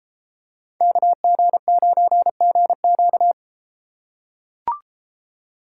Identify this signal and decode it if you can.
Morse code, heard as a keyed tone, about 33 wpm, 700 Hz. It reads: KG9GQ